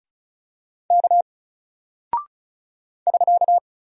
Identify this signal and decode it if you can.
Morse code, heard as a keyed tone, about 36 wpm, 700 Hz. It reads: K <SK>
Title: Single Character (Letter:Number:Punctuation:Prosign) Alt 35wpm